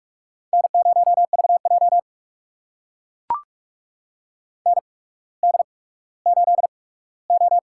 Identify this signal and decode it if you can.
Morse code, heard as a keyed tone, about 44 wpm, spaced as for 19 wpm, 700 Hz. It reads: N0VJ ND8O
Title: USA Call Signs - 4 Characters - 8x Character Spacing - 45wpm